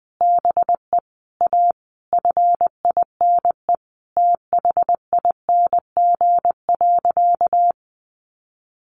Morse code, heard as a keyed tone, about 20 wpm, 700 Hz. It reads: BE A FINE THING.
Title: Sentences from Top 500 Words 20wpm